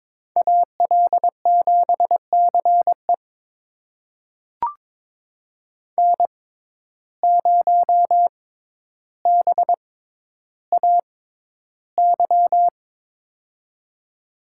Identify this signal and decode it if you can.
Morse code, heard as a keyed tone, about 22 wpm, spaced as for 11 wpm, 700 Hz. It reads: AL7CE N0BAY
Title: USA Call Signs - 5 Characters - 6x Character Spacing - 22wpm